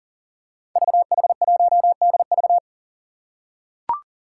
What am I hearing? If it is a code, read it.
Morse code, heard as a keyed tone, about 40 wpm, 700 Hz. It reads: VH1DV
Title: Five Challenging Letter-Number Combinations 40wpm